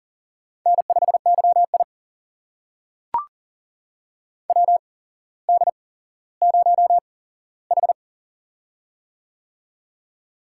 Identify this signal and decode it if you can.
Morse code, heard as a keyed tone, about 39 wpm, spaced as for 17 wpm, 700 Hz. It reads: N5YI WD0H
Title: USA Call Signs - 4 Characters - 8x Character Spacing - 40wpm